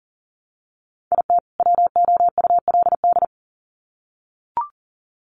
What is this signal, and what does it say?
Morse code, heard as a keyed tone, about 40 wpm, 700 Hz. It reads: IT WOULD
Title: Sets of 2 Words 40wpm